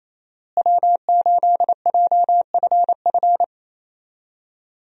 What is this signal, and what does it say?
Morse code, heard as a keyed tone, about 27 wpm, 700 Hz. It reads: W8JFF
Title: USA Call Signs - 5 Characters - Repeated 3x - 28wpm